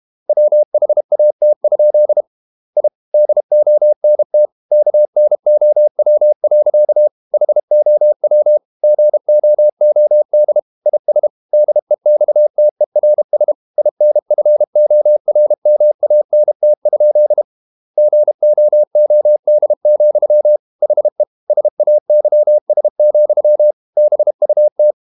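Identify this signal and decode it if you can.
Morse code, heard as a keyed tone, about 32 wpm, 600 Hz. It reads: WHAT? I DONT KNOW. HOW GOOD IS DEXTERS INFORMANT? GOOD, HE SAYS, BUT
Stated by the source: Morse code practice at 32 wpm from CW med Gnister, reading Ken Follett's novel Never in English.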